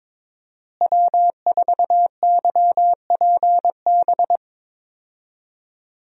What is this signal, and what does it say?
Morse code, heard as a keyed tone, about 22 wpm, 700 Hz. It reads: W4YPB
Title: USA Call Signs - 5 Characters - Repeated 2x - 22wpm